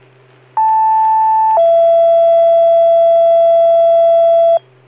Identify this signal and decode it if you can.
Two-tone page, heard as a one-second tone, then a three-second tone, about 880 then 670 Hz. It is heard from 0.6 to 4.6 s.